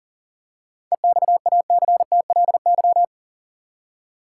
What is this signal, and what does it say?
Morse code, heard as a keyed tone, about 40 wpm, 700 Hz. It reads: EXACTLY
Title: Seven Letter Words - 40wpm